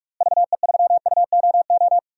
Morse code, heard as a keyed tone, about 45 wpm, 700 Hz. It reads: VE3UOO
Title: DXCC-100 Call Signs - 6 Characters - Repeated 3x - 45wpm